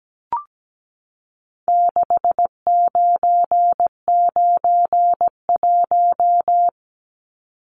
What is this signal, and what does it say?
Morse code, heard as a keyed tone, about 17 wpm, 700 Hz. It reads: 6991
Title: Numbers - 4-Digit 17wpm